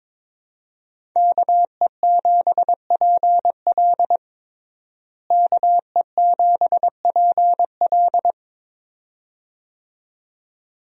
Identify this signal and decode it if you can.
Morse code, heard as a keyed tone, about 21 wpm, 700 Hz. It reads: KE7PL KE7PL
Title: USA Call Signs - 5 Characters - Repeated 2x - 22wpm